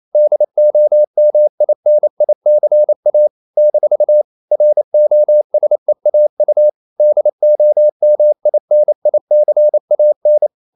Morse code, heard as a keyed tone, about 28 wpm, 600 Hz. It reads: DOMINICA - ROSEAU DOMINICAN
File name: CW / Practice / NationalCapitals28